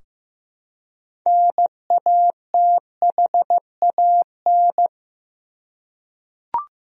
Morse code, heard as a keyed tone, about 15 wpm, 700 Hz. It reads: NATHAN
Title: USA Common Names - 15wpm